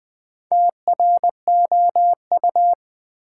Morse code, heard as a keyed tone, about 20 wpm, 700 Hz. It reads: TROU
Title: Build A Word - 4 Letters At A Time - 8-Letter Words - 20wpm